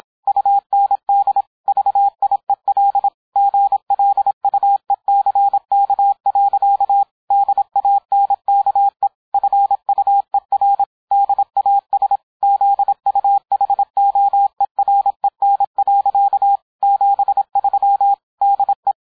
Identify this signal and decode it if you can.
Morse code, heard as a keyed tone, about 26 wpm, 800 Hz. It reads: UND VIEL GLUECK. DANKE FUER DAS ZUHOEREN. 73 DE